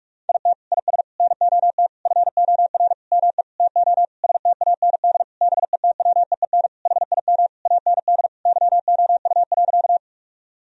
Morse code, heard as a keyed tone, about 41 wpm, 700 Hz. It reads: IT IS NOT FOR ME TO STAND BETWEEN HIM AND YOU.
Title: Sentences from Top 1000 Words - Review Part 1 - 45wpm